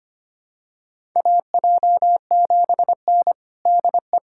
Morse code, heard as a keyed tone, about 25 wpm, 700 Hz. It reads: AJ7N DE
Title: USA Callsign DE Callsign - 4 Characters - 25wpm